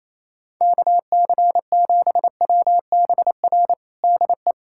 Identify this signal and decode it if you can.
Morse code, heard as a keyed tone, about 28 wpm, 700 Hz. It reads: KC7WBR DE
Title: USA Callsign DE Callsign - 6 Characters - 28wpm